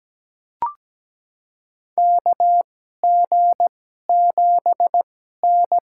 Morse code, heard as a keyed tone, about 17 wpm, spaced as for 12 wpm, 700 Hz. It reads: KG7N
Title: USA Call Signs - 2x Character Spacing - 17wpm